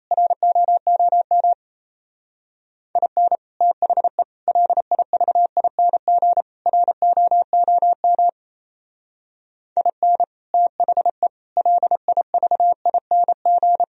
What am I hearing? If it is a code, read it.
Morse code, heard as a keyed tone, about 34 wpm, 700 Hz. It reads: ROOM IN THE LIVING ROOM IN THE LIVING
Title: Speed-Racing - Sets of 4 Words 28wpm